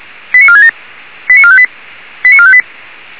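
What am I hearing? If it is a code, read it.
Five-tone selective call (ZVEI), heard as quick runs of five tones, tones of about 70 ms. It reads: 89467 89458 89458
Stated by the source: ZVEI five-tone selective call 89549 received on an SDR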